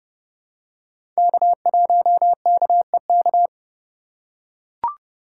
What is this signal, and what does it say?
Morse code, heard as a keyed tone, about 29 wpm, 700 Hz. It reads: K1KEK
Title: USA Call Signs - 5 Characters - 3x Character Spacing - 30wpm